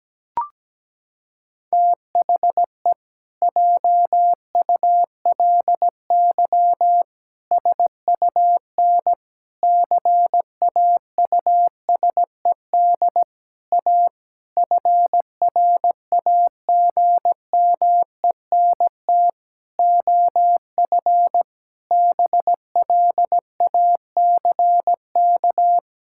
Morse code, heard as a keyed tone, about 17 wpm, 700 Hz. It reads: THE JULY SUN CAUSED A FRAGMENT OF BLACK
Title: Pangram Challenge v2 - 17wpm